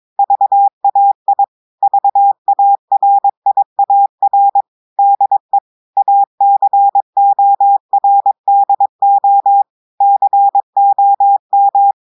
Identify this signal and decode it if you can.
Morse code, heard as a keyed tone, about 22 wpm, 800 Hz. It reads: VAI VARIAR DE ACORDO COM
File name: audios / portugais / portugais texte en clair 2 vitesse 22 mots minutes